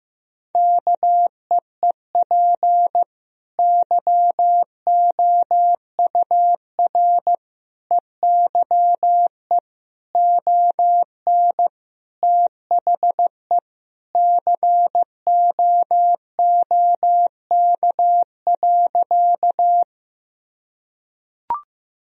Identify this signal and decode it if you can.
Morse code, heard as a keyed tone, about 16 wpm, 700 Hz. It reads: KEEP YOUR EYE ON THE COOK.
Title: Sentences from Top 1000 Words - Review Part 4 - 15wpm